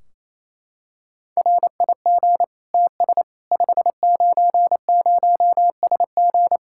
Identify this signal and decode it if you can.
Morse code, heard as a keyed tone, about 28 wpm, 700 Hz. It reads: RIG TS 590SG
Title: Practice QSO - Exchange 2 - Info - 28wpm